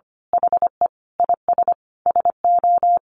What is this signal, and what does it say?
Morse code, heard as a keyed tone, about 25 wpm, 700 Hz. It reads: HE IS SO
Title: Sentences from Top 500 Words 25wpm